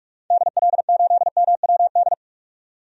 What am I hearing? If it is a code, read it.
Morse code, heard as a keyed tone, about 46 wpm, 700 Hz. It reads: DL8MWD
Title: DXCC-100 Call Signs - 6 Characters - 45wpm